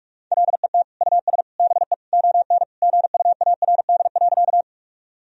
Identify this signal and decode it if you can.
Morse code, heard as a keyed tone, about 44 wpm, 700 Hz. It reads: LET US BE ON GUARD.
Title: Sentences from Top 1000 Words - Review Part 4 - 45wpm